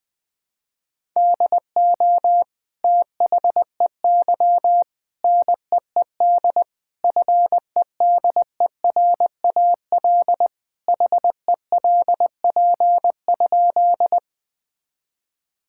Morse code, heard as a keyed tone, about 21 wpm, 700 Hz. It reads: DO THEY NEED FEDERAL HELP?
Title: Sentences from Top 1000 Words - Review Part 5 - 20wpm